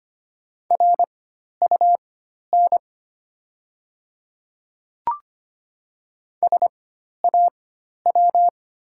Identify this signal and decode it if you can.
Morse code, heard as a keyed tone, about 25 wpm, spaced as for 12 wpm, 700 Hz. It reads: RUN SAW
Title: 4x Character Spacing - Top 300 Words - 25wpm